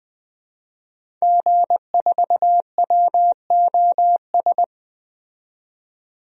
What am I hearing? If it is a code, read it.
Morse code, heard as a keyed tone, about 20 wpm, 700 Hz. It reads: G4WOS
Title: DXCC-100 Call Signs - 5 Characters - 20wpm